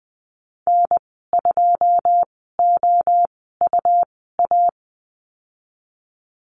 Morse code, heard as a keyed tone, about 20 wpm, spaced as for 15 wpm, 700 Hz. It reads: N2OUA
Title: 193 Lesson 31g - Letter v - Call Signs 20wpm